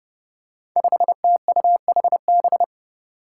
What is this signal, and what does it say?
Morse code, heard as a keyed tone, about 28 wpm, 700 Hz. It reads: 5TUHB
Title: Five Challenging Letter-Number Combinations 30wpm